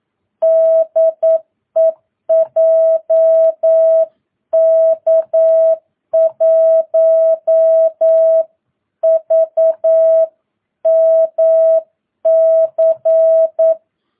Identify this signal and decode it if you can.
Morse code, heard as a keyed tone, about 9 wpm, 650 Hz. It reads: DE JK1VMC
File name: jh8cbh / 2022cw